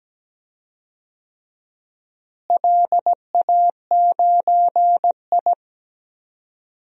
Morse code, heard as a keyed tone, about 17 wpm, 700 Hz. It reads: LA9I
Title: DXCC-100 Call Signs - 4 Characters - Repeated 3x - 17wpm